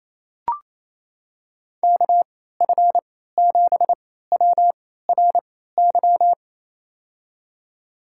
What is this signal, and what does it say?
Morse code, heard as a keyed tone, about 27 wpm, spaced as for 16 wpm, 700 Hz. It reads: KF7WRY
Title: USA Call Signs - 3x Character Spacing - 28wpm